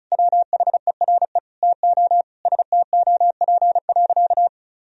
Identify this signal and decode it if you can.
Morse code, heard as a keyed tone, about 34 wpm, 700 Hz. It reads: WHERE TO STOP.
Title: Sentences from Top 1000 Words - Review Part 2 - 35wpm